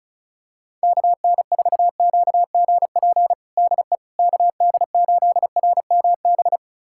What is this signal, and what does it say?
Morse code, heard as a keyed tone, about 36 wpm, 700 Hz. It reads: KN4QGP DE KD8RMB
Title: USA Callsign DE Callsign 35wpm